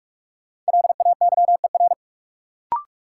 Morse code, heard as a keyed tone, about 42 wpm, 700 Hz. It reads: LAYER
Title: Sentences from Top 1600 Words - New Word Drill - 45wpm